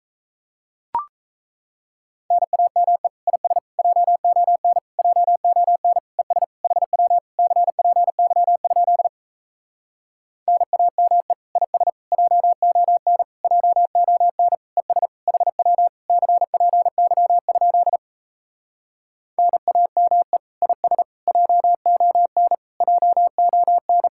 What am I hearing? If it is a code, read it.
Morse code, heard as a keyed tone, about 37 wpm, 700 Hz. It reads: NAME IS JON JON ES HW CPY? NAME IS JON JON ES HW CPY? NAME IS JON JON